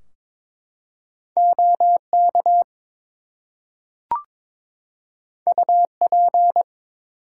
Morse code, heard as a keyed tone, about 22 wpm, 700 Hz. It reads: OK UP
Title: Two Letter Words - 22wpm